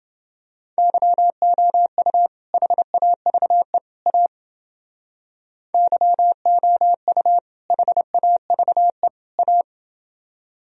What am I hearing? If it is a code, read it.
Morse code, heard as a keyed tone, about 29 wpm, 700 Hz. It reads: YOU HAVE A YOU HAVE A